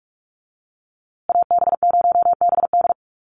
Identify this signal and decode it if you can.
Morse code, heard as a keyed tone, about 44 wpm, 700 Hz. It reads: AB0BD
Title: USA Call Signs Repeated 2x 45wpm